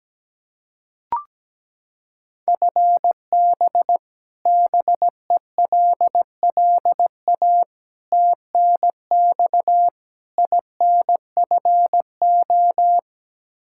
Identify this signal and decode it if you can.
Morse code, heard as a keyed tone, about 17 wpm, 700 Hz. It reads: FB BELLA TNX INFO
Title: Practice QSO - Exchange 3 - More Info - 17wpm